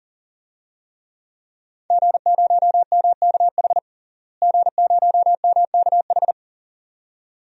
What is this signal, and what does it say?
Morse code, heard as a keyed tone, about 40 wpm, 700 Hz. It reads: G0MKH G0MKH
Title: DXCC-100 Call Signs - 5 Characters - Repeated 2x - 40wpm